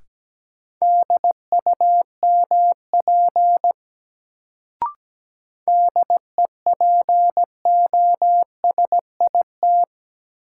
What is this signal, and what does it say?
Morse code, heard as a keyed tone, about 17 wpm, 700 Hz. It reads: DUMP DEPOSIT
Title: Sentences from Top 1300 Words - New Word Drill Alt - 17wpm